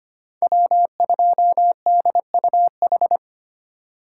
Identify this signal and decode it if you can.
Morse code, heard as a keyed tone, about 25 wpm, 700 Hz. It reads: W2DUH
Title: USA Call Signs - 5 Characters - 2x Character Spacing - 25wpm